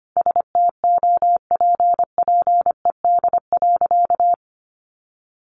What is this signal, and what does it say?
Morse code, heard as a keyed tone, about 25 wpm, 700 Hz. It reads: STOPPED.
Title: A Sound Of Thunder 25wpm